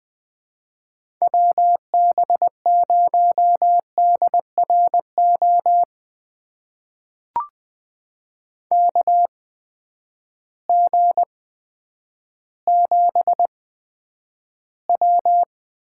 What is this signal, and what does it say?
Morse code, heard as a keyed tone, about 21 wpm, spaced as for 10 wpm, 700 Hz. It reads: WB0DRO KG7W
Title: USA Call Signs - 6 Characters - 8x Character Spacing - 20wpm